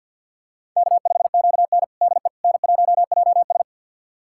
Morse code, heard as a keyed tone, about 46 wpm, 700 Hz. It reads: K5QN DE N1JS
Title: USA Callsign DE Callsign - 4 Characters - 50wpm